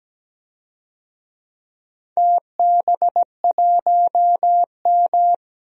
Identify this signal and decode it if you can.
Morse code, heard as a keyed tone, about 17 wpm, 700 Hz. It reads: TB1M
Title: DXCC-100 Call Signs - 4 Characters - 17wpm